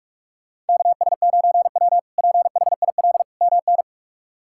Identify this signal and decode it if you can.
Morse code, heard as a keyed tone, about 46 wpm, 700 Hz. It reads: KS9W PHIL MN
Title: K1USN SST QSO Practice - 45wpm